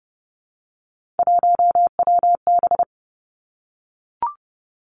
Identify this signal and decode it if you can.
Morse code, heard as a keyed tone, about 32 wpm, 700 Hz. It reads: 1WB
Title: Three Challenging Letter-Number Combinations 30wpm